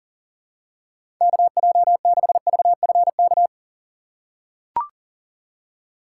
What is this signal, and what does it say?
Morse code, heard as a keyed tone, about 40 wpm, 700 Hz. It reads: KJ6VFK
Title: USA Call Signs - 6 Characters - 2x Character Spacing - 40wpm